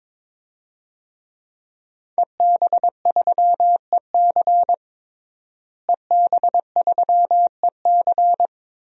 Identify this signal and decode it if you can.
Morse code, heard as a keyed tone, about 22 wpm, 700 Hz. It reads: EB3EC EB3EC
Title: DXCC-100 Call Signs - 5 Characters - Repeated 2x - 22wpm